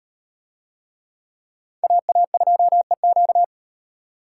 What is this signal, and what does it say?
Morse code, heard as a keyed tone, about 38 wpm, 700 Hz. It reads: AA2EQ